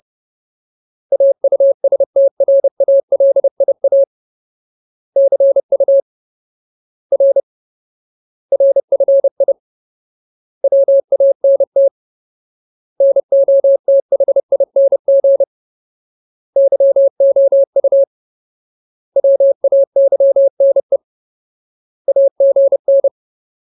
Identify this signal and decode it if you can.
Morse code, heard as a keyed tone, about 30 wpm, 550 Hz. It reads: AUSTRALIA CU R RFI WANT NOTHING YOU WAYNE AGN
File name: Ultimate Word List / 30WPM Ultimate